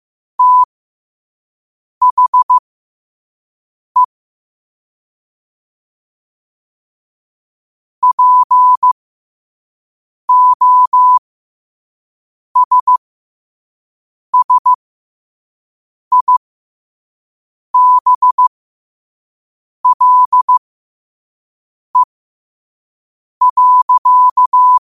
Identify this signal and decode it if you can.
Morse code, heard as a keyed tone, about 15 wpm, spaced as for 5 wpm, 1000 Hz. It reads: THE POSSIBLE.
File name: 5wpm